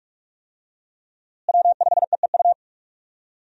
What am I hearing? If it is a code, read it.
Morse code, heard as a keyed tone, about 45 wpm, 700 Hz. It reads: W5EEU
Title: USA Call Signs - 5 Characters - Repeated 2x - 45wpm